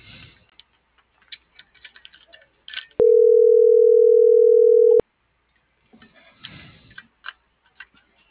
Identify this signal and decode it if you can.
Ringback tone (North America), 440 + 480 Hz.